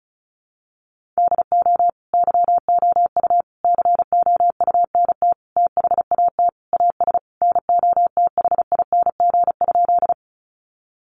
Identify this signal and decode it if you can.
Morse code, heard as a keyed tone, about 34 wpm, 700 Hz. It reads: DO YOU COUNT THAT AS NOTHING?
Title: Sentences from Top 1500 Words 35wpm